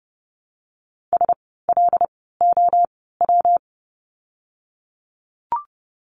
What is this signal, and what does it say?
Morse code, heard as a keyed tone, about 30 wpm, 700 Hz. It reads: S L O W
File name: course / 30:10wpm / 115 Lesson 19f - Letter w - Words 30wpm